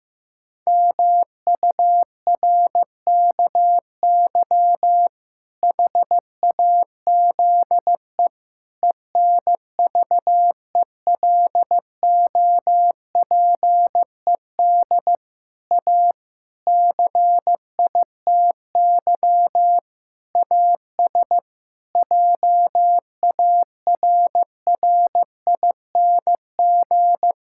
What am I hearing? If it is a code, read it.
Morse code, heard as a keyed tone, about 15 wpm, 700 Hz. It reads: MURKY HAZE ENVELOPED A CITY AS JARRING
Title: Pangram Challenge v2 - 15wpm